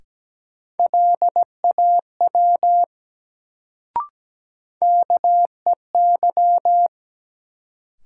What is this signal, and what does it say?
Morse code, heard as a keyed tone, about 17 wpm, 700 Hz. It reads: LAW KEY
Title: Sentences from Top 600 Words - New Word Drill - 17wpm